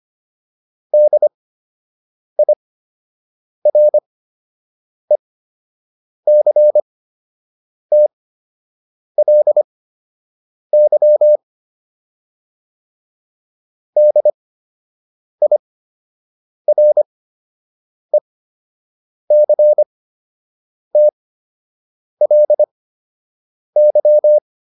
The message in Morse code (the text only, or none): DIRECTLY DIRECTLY